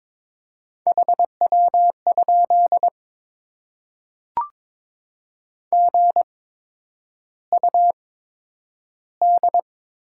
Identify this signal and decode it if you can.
Morse code, heard as a keyed tone, about 21 wpm, spaced as for 9 wpm, 700 Hz. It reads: HW? GUD
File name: Top 20 QSO Elements - 8x Character Spacing - 22wpm